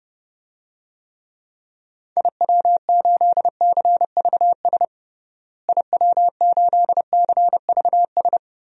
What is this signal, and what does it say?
Morse code, heard as a keyed tone, about 32 wpm, 700 Hz. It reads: IW8CVS IW8CVS
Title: DXCC-100 Call Signs - 6 Characters - Repeated 2x - 30wpm